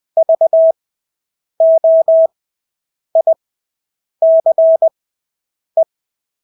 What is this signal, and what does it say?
Morse code, heard as a keyed tone, about 20 wpm, spaced as for 8 wpm, 650 Hz. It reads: VOICE
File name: Engelske ord 20/8 wpm